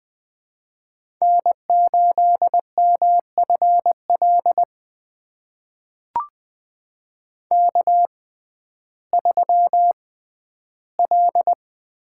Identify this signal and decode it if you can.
Morse code, heard as a keyed tone, about 20 wpm, spaced as for 12 wpm, 700 Hz. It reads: N8MFL K3L
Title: USA Call Signs - 5 Characters - 6x Character Spacing - 20wpm